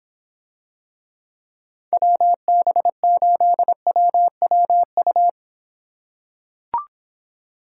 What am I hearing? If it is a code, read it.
Morse code, heard as a keyed tone, about 26 wpm, 700 Hz. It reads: WB8WWU